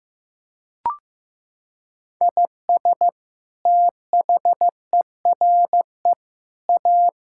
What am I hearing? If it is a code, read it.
Morse code, heard as a keyed tone, about 15 wpm, 700 Hz. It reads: IS THERE A
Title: Sentences from Top 1000 Words - Review Part 6 - 15wpm